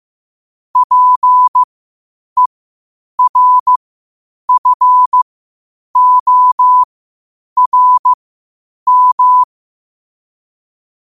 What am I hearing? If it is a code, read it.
Morse code, heard as a keyed tone, about 15 wpm, spaced as for 8 wpm, 1000 Hz. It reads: PERFORM
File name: qotdfiles / 8wpm